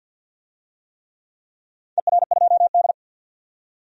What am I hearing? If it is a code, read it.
Morse code, heard as a keyed tone, about 50 wpm, 700 Hz. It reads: ER2D
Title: DXCC-100 Call Signs - 4 Characters - 50wpm